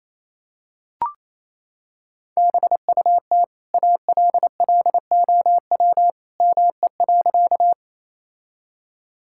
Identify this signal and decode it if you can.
Morse code, heard as a keyed tone, about 28 wpm, 700 Hz. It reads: BUT ALLOW ME.
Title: Sentences from Top 1000 Words - Review Part 5 - 28wpm